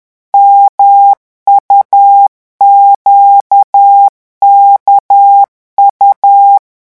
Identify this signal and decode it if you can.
Morse code, heard as a keyed tone, about 10 wpm, 800 Hz. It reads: MUQKU
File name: morse mp3 / revision2-vitesse 10 mots